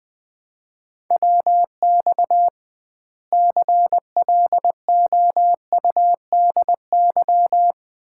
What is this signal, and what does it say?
Morse code, heard as a keyed tone, about 21 wpm, 700 Hz. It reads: WX CLOUDY
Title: Common QSO Phrases - 2x Word-Spacing - 20wpm